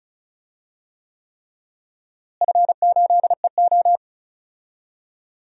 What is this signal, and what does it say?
Morse code, heard as a keyed tone, about 36 wpm, 700 Hz. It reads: F8EO
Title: DXCC-100 Call Signs - 4 Characters - Repeated 3x - 35wpm